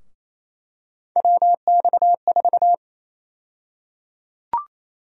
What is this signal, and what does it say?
Morse code, heard as a keyed tone, about 28 wpm, 700 Hz. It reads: WX4
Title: USA Call Sign Prefixes - Repeated 2x - 28wpm